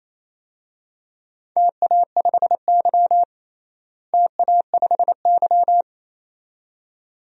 Morse code, heard as a keyed tone, about 28 wpm, 700 Hz. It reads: TA5Y TA5Y
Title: DXCC-100 Call Signs - 4 Characters - Repeated 2x - 28wpm